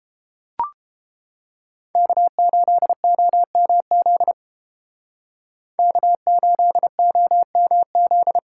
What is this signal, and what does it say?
Morse code, heard as a keyed tone, about 30 wpm, 700 Hz. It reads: K8OMZ K8OMZ